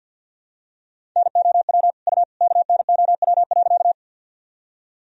Morse code, heard as a keyed tone, about 49 wpm, 700 Hz. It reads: NOW U KNOW.
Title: Sentences from Top 100 Words and Abbreviations 50wpm